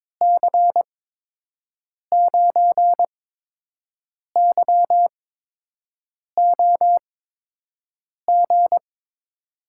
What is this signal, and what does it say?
Morse code, heard as a keyed tone, about 22 wpm, spaced as for 6 wpm, 700 Hz. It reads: C9YOG
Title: USA Call Signs - 6 Characters - 8x Character Spacing - 22wpm